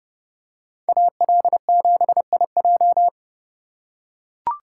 Morse code, heard as a keyed tone, about 30 wpm, 700 Hz. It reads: AL7IJ